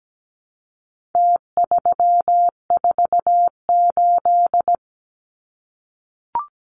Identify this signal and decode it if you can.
Morse code, heard as a keyed tone, about 17 wpm, 700 Hz. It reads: T348
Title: Numbers - Contest Style - 4-Digit 17wpm